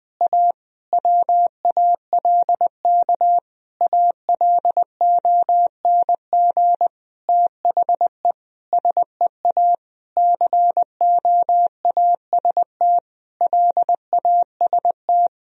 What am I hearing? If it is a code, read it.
Morse code, heard as a keyed tone, about 20 wpm, 700 Hz. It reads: A WALK ALONG THE SEA COAST LAST
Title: Sentences from Top 1000 Words - Review Part 4 - 20wpm